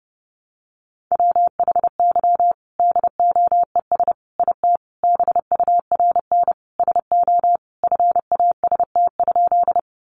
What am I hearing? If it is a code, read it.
Morse code, heard as a keyed tone, about 30 wpm, 700 Hz. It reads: WHY DOES IT BURN SO FAST?
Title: Sentences from Top 1300 Words 30wpm